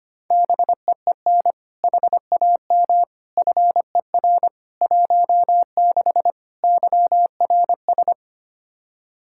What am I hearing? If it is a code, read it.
Morse code, heard as a keyed tone, about 25 wpm, 700 Hz. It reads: BEEN HAM FER 16 YRS
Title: K1USN SST QSO Practice 25wpm